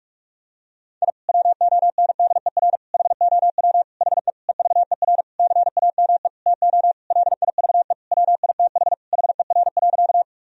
Morse code, heard as a keyed tone, about 41 wpm, 700 Hz. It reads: I WONDER HOW HE EVER CAME TO LIVE WITH HER.
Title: Sentences from Top 1000 Words - Review Part 3 - 45wpm